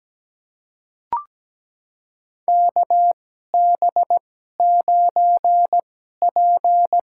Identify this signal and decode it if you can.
Morse code, heard as a keyed tone, about 17 wpm, spaced as for 12 wpm, 700 Hz. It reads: KB9P
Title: USA Call Signs - 2x Character Spacing - 17wpm